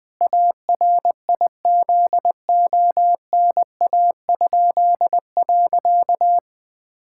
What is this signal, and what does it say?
Morse code, heard as a keyed tone, about 20 wpm, 700 Hz. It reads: ARIZONA?.
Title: Pangram Challenge v2 - 20wpm